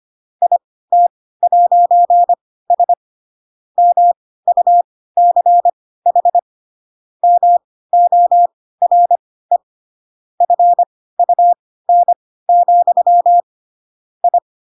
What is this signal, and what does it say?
Morse code, heard as a keyed tone, about 25 wpm, spaced as for 16 wpm, 700 Hz. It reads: IT'S MUCH MORE FUN, I
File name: Pinocchio / Pinocchio ch4 10 25 ik1hge